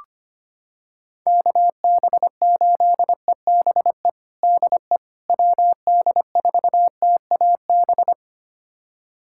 Morse code, heard as a keyed tone, about 25 wpm, 700 Hz. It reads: KB8EBE DE WD4TAB
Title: USA Callsign DE Callsign - 6 Characters - 25wpm